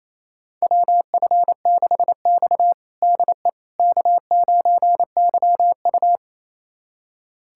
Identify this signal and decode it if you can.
Morse code, heard as a keyed tone, about 28 wpm, 700 Hz. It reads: WF6X DE K9YU
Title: USA Callsign DE Callsign - 4 Characters - 28wpm